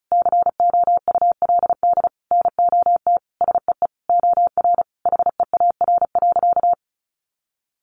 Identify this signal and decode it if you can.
Morse code, heard as a keyed tone, about 34 wpm, 700 Hz. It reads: COULD NOT SEE OR HEAR.
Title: Sentences from Top 600 Words 35wpm